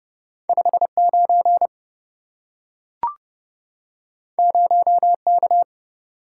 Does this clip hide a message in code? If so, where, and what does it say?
Yes: Morse code from 0.5 to 5.6 s, heard as a keyed tone, about 27 wpm, 700 Hz. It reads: 59 0K